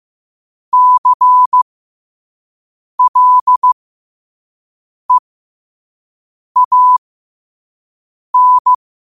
Morse code, heard as a keyed tone, about 15 wpm, spaced as for 5 wpm, 1000 Hz. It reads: CLEAN
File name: qotdfiles / 5wpm